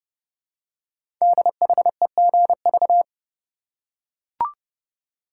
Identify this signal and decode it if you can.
Morse code, heard as a keyed tone, about 30 wpm, 700 Hz. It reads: DHEGV
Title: Code Groups - 5 Letters - 30wpm